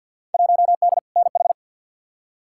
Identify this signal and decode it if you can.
Morse code, heard as a keyed tone, about 53 wpm, 700 Hz. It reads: 1D NH
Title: Field Day Exchanges Encore - 50wpm